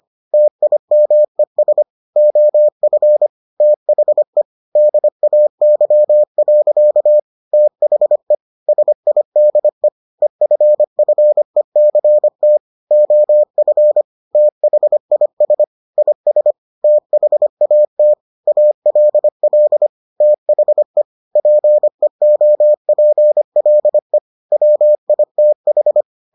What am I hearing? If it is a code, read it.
Morse code, heard as a keyed tone, about 25 wpm, 600 Hz. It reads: TIMES OF THE DAY. THE SIDE EFFECT OF THIS IS THAT ALL THE PEOPLE WITH